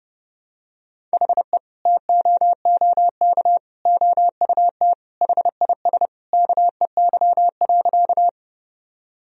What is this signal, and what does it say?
Morse code, heard as a keyed tone, about 30 wpm, 700 Hz. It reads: HE TOOK OUT HIS KEY.
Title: Sentences from Top 1000 Words - Review Part 4 - 30wpm